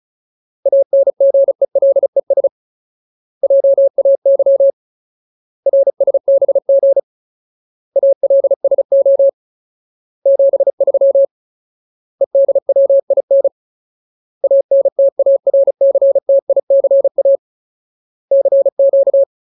Morse code, heard as a keyed tone, about 34 wpm, 550 Hz. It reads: ANGELES JAY RSBG ALSO 73 EDWIN ANTARCTICA CQ
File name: Ultimate Word List / 35WPM Ultimate